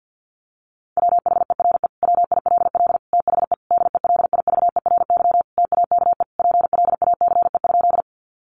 Morse code, heard as a keyed tone, about 48 wpm, 700 Hz. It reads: WHERE WILL THE DELIVERY TAKE PLACE?
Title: Sentences from Top 1500 Words 50wpm